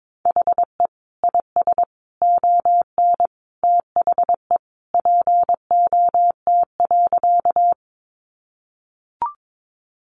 Morse code, heard as a keyed tone, about 22 wpm, 700 Hz. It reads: HE IS ON THE POT.
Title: Sentences from Top 1000 Words 22wpm